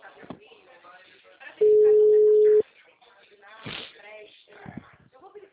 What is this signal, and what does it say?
Ringback tone (Europe, 425 Hz), with speech.
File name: out- -2007- - - .18495